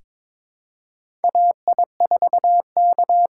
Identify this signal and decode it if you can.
Morse code, heard as a keyed tone, about 22 wpm, 700 Hz. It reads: AI4K
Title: USA Call Signs - 5 Characters - 6x Character Spacing - 22wpm